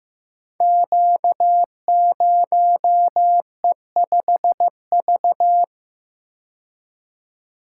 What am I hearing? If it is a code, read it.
Morse code, heard as a keyed tone, about 15 wpm, 700 Hz. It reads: Q0E5V
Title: Code Groups - 5 Letter:Numbers - 15wpm